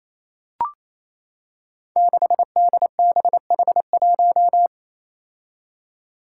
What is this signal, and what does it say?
Morse code, heard as a keyed tone, about 29 wpm, 700 Hz. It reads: 6DBH1
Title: Five Challenging Letter-Number Combinations 28wpm